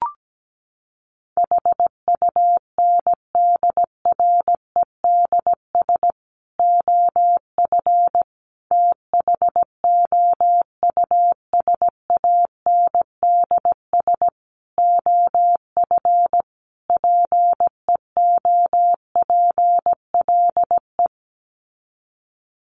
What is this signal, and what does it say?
Morse code, heard as a keyed tone, about 17 wpm, 700 Hz. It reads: HUNDREDS OF THOUSANDS OF PEOPLE
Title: Sets of 5 Words Encore2 17wpm